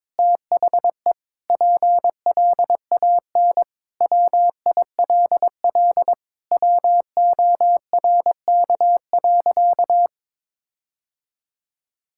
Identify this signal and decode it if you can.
Morse code, heard as a keyed tone, about 22 wpm, 700 Hz. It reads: THE PLAN WILL WORK.
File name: practice / 22wpm / Sentences with 4-Word Phrases - 22wpm